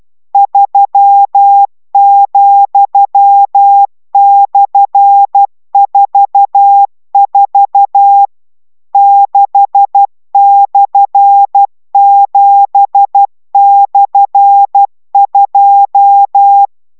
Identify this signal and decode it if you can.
Morse code, heard as a keyed tone, about 12 wpm, 800 Hz. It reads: 3,/44 6/7/2